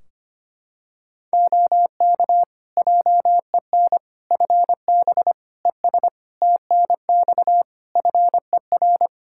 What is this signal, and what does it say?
Morse code, heard as a keyed tone, about 25 wpm, 700 Hz. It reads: OK JEN FB ES TNX FER
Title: Practice QSO - Exchange 2 - Info - 25wpm